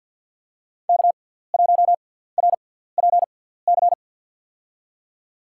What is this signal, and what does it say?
Morse code, heard as a keyed tone, about 53 wpm, spaced as for 17 wpm, 700 Hz. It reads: K1RPC
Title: USA Call Signs - 5 Characters - 6x Character Spacing - 50wpm